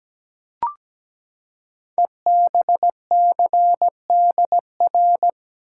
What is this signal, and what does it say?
Morse code, heard as a keyed tone, about 17 wpm, 700 Hz. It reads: EBCDR
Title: Code Groups - 5 Letters - 17wpm